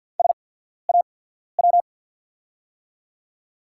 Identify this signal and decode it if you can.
Morse code, heard as a keyed tone, about 49 wpm, spaced as for 14 wpm, 700 Hz. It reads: SAW
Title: 8x Character Spacing - Top 300 Words - 50wpm